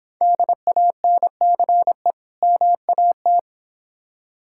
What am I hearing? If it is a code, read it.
Morse code, heard as a keyed tone, about 26 wpm, 700 Hz. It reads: DANCE MAT